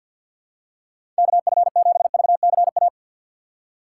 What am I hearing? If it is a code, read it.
Morse code, heard as a keyed tone, about 50 wpm, 700 Hz. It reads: KF7VCA
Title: USA Call Signs - 6 Characters - 6x Character Spacing - 50wpm